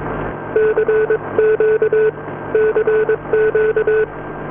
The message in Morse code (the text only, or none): CQ CQ